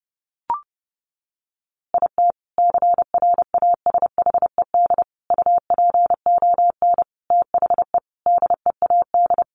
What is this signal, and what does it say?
Morse code, heard as a keyed tone, about 30 wpm, 700 Hz. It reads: IT CRASHED UPON THE DEAD
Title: A Sound Of Thunder 30wpm